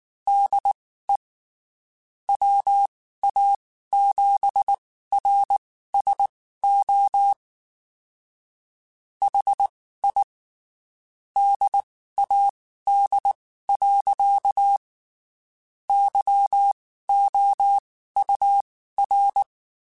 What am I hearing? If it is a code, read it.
Morse code, heard as a keyed tone, about 19 wpm, spaced as for 14 wpm, 800 Hz. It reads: DE WA7RSO HI DAD. YOUR
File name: (spaced) / QSOs Spaced-mp3s / WD0FBT @18 sp